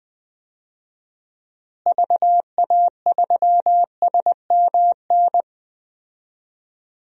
Morse code, heard as a keyed tone, about 21 wpm, 700 Hz. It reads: VA3SMN
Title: DXCC-100 Call Signs - 6 Characters - Repeated 2x - 20wpm